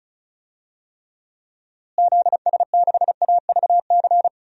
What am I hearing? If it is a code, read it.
Morse code, heard as a keyed tone, about 36 wpm, 700 Hz. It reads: ZS6AVC
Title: DXCC-100 Call Signs - 6 Characters - Repeated 3x - 35wpm